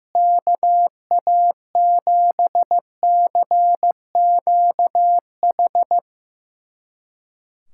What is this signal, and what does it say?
Morse code, heard as a keyed tone, about 16 wpm, 700 Hz. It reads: KA7CQH
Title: USA Callsign DE Callsign 15wpm